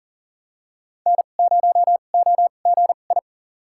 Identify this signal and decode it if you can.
Morse code, heard as a keyed tone, about 40 wpm, spaced as for 29 wpm, 700 Hz. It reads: N0OGI
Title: USA Call Signs - 5 Characters - 2x Character Spacing - 40wpm